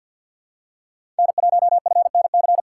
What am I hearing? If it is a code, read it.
Morse code, heard as a keyed tone, about 53 wpm, 700 Hz. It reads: N1FNC
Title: USA Call Signs - 5 Characters - 8x Character Spacing - 50wpm